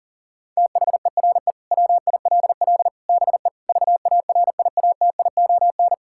Morse code, heard as a keyed tone, about 37 wpm, 700 Hz. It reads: THERE WILL BE VARIATION
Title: Sentences from Top 1600 Words 40wpm